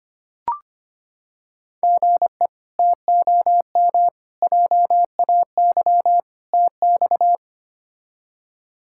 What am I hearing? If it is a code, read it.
Morse code, heard as a keyed tone, about 25 wpm, 700 Hz. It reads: GE TOM JAY TX
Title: K1USN SST QSO Practice - 25wpm